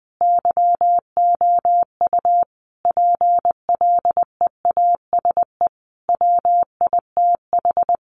Morse code, heard as a keyed tone, about 20 wpm, 700 Hz. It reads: YOU PLEASE WITH
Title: Sentences from Top 1500 Words 20wpm